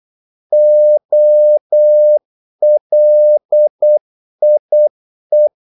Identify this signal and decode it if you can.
Morse code, heard as a keyed tone, about 8 wpm, 600 Hz. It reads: OLIE
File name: Korte ord DK 8wpm